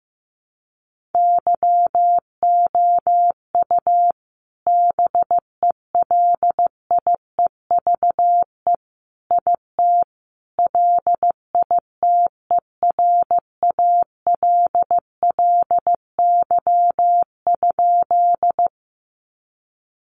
Morse code, heard as a keyed tone, about 15 wpm, 700 Hz. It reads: YOU BELIEVE IT LITERALLY?
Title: Sentences from Top 1400 Words 15wpm